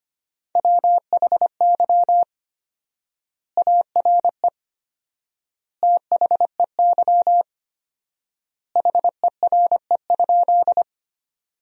Morse code, heard as a keyed tone, about 25 wpm, 700 Hz. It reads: WHY ARE THEY HERE?
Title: 4x Word-Spacing - Sentences from Top 200 Words - 25wpm